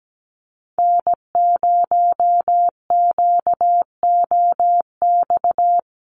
Morse code, heard as a keyed tone, about 17 wpm, 700 Hz. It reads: N0QOX
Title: USA Call Signs Repeated 3x 17wpm